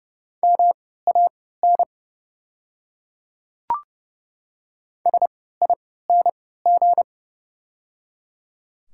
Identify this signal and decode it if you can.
Morse code, heard as a keyed tone, about 30 wpm, spaced as for 17 wpm, 700 Hz. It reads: MAN SING
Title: 3x Character Spacing - Top 500 Words - 30wpm